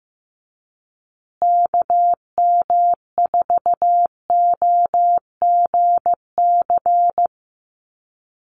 Morse code, heard as a keyed tone, about 15 wpm, 700 Hz. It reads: KM4OGC
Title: USA Call Signs Repeated 3x 15wpm